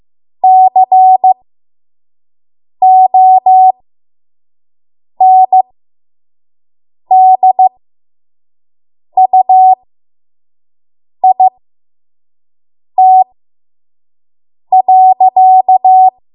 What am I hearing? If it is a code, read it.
Morse code, heard as a keyed tone, about 15 wpm, spaced as for 5 wpm, 750 Hz. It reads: CONDUIT.
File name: Morse / 05WPM